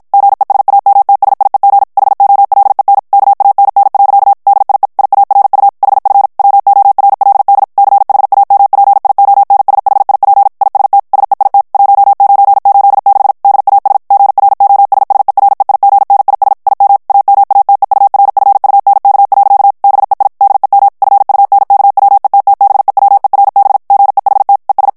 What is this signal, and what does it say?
Morse code, heard as a keyed tone, about 53 wpm, 800 Hz. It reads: GESAMTSIEG HOLEN KANN. DIE IARU HF WORLD CHAMPIONSHIP IST SEIT 1986 DAS GROSSEREIGNIS IM AMATEURFUNK. BEI DEM FUNKWETTBEWERB GEHT ES
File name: morse mp3 / ALLEMAND texte clair 2 vitesse 50 mots minute